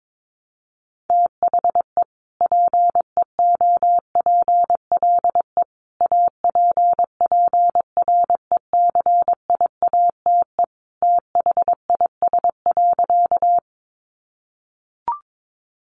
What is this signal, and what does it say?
Morse code, heard as a keyed tone, about 22 wpm, 700 Hz. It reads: THE PEOPLE APPRECIATE THIS.
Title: Sentences from Top 1300 Words 22wpm